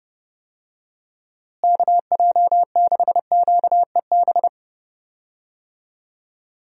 Morse code, heard as a keyed tone, about 31 wpm, 700 Hz. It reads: KJ6QEB
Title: USA Call Signs - 6 Characters - Repeated 2x - 30wpm